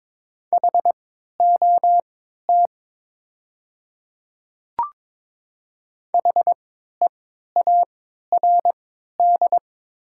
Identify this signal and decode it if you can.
Morse code, heard as a keyed tone, about 22 wpm, spaced as for 13 wpm, 700 Hz. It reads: HOT HEARD
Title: 3x Character Spacing - Top 500 Words - 22wpm